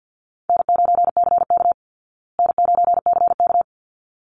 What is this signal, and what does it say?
Morse code, heard as a keyed tone, about 48 wpm, 700 Hz. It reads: N9CK N9CK
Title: Call Signs Repeated 3x - 50wpm